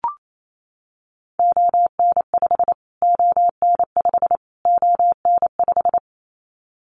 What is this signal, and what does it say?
Morse code, heard as a keyed tone, about 28 wpm, 700 Hz. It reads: ON5 ON5 ON5
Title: Callsign Prefixes for Top 25 DXCC - Repeated 3x - 28wpm